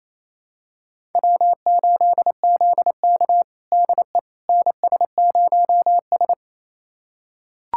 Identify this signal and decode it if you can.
Morse code, heard as a keyed tone, about 27 wpm, 700 Hz. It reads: W8ZK DE NS0S
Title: USA Callsign DE Callsign - 4 Characters - 28wpm